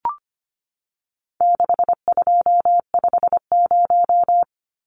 Morse code, heard as a keyed tone, about 25 wpm, 700 Hz. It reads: 6250
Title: Numbers - 4-Digit 25wpm